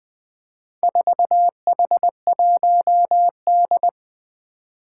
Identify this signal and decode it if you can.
Morse code, heard as a keyed tone, about 20 wpm, 700 Hz. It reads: 4H1D
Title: Four Challenging Letter-Number Combinations 20wpm